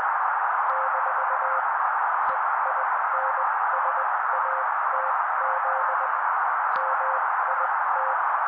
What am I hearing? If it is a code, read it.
Morse code, heard as a keyed tone, about 20 wpm, spaced as for 16 wpm, 530 Hz. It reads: - EINSATZ MIT
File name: daily / 16wpm noise